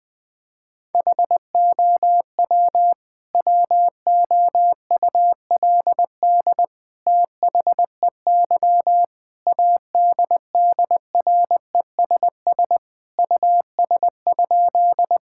Morse code, heard as a keyed tone, about 20 wpm, 700 Hz. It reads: HOW WOULD THEY ADDRESS US?
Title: Sentences from Top 1000 Words - Review Part 6 - 20wpm